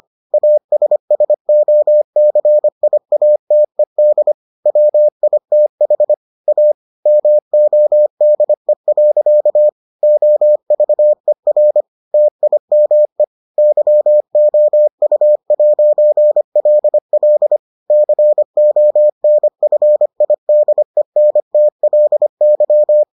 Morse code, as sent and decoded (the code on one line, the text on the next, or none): .- ... ... --- -.-. .. .- - . -.. / .-- .. - .... / .- / -- --- -.. .-.-.- / --- ...- . .-. / - .. -- . / -.-- --- ..- .----. .-.. .-.. / -.-. --- -. ..-. .. -.. -. - .-.. -.--
ASSOCIATED WITH A MODE. OVER TIME YOU'LL CONFIDENTLY